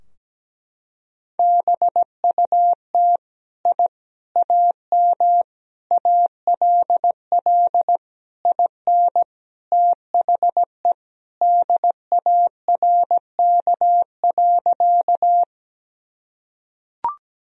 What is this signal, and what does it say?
Morse code, heard as a keyed tone, about 17 wpm, 700 Hz. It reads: BUT I AM ALL IN THE DARK.
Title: Sentences from Top 1000 Words - Review Part 2 - 17wpm